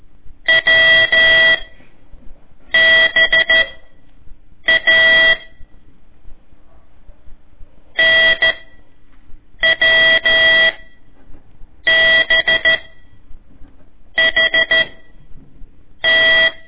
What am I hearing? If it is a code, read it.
Morse code, heard as a keyed tone, about 13 wpm, spaced as for 6 wpm, 1950 Hz. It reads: WBA NWB HT
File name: ik1weg2 / CORSI CW / lez4-1